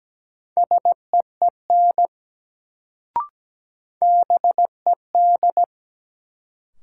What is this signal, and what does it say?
Morse code, heard as a keyed tone, about 17 wpm, 700 Hz. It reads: SEEN BED
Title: Sets of 5 Words Encore - New Word Drill - 17wpm